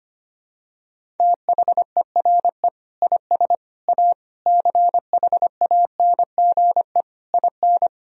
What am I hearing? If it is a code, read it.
Morse code, heard as a keyed tone, about 25 wpm, 700 Hz. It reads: THERE IS A CHANGE IN
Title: Sentences from Top 1000 Words - Review Part 1 - 25wpm